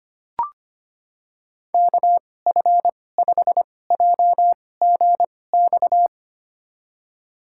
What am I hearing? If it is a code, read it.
Morse code, heard as a keyed tone, about 25 wpm, spaced as for 18 wpm, 700 Hz. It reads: KF5JGX
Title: USA Call Signs - 6 Characters - 2x Character Spacing - 25wpm